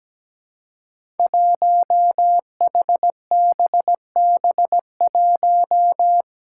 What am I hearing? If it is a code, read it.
Morse code, heard as a keyed tone, about 17 wpm, 700 Hz. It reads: 1HBB1
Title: Five Challenging Letter-Number Combinations 17wpm